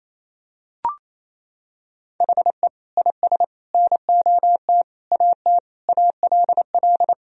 Morse code, heard as a keyed tone, about 28 wpm, 700 Hz. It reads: HE IS NOT AT ALL
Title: Sentences from Top 1000 Words - Review Part 4 - 28wpm